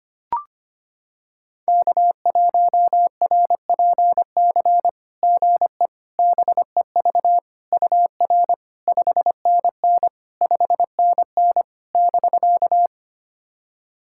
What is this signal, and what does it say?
Morse code, heard as a keyed tone, about 25 wpm, 690 Hz. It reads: K1RPC GE BEV UR 5NN 5NN <BK>
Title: SOTA QSO Practice 25wpm